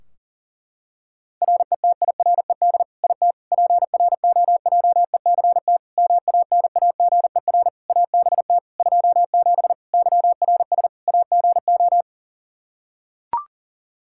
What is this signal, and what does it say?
Morse code, heard as a keyed tone, about 40 wpm, 700 Hz. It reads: RETIRED IT PROJECT MANAGER ABT 27 YRS AGO
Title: Practice QSO - Exchange 3 - More Info - 40wpm